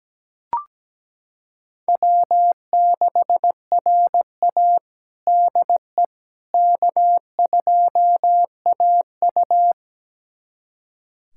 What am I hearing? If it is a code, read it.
Morse code, heard as a keyed tone, about 17 wpm, 700 Hz. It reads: W6RA DE K2AU
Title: USA Callsign DE Callsign - 4 Characters - 17wpm